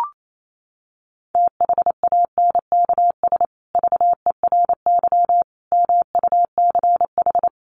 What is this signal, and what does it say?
Morse code, heard as a keyed tone, about 28 wpm, 700 Hz. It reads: THANKS VERY MUCH
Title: Sets of 5 Words Encore2 28wpm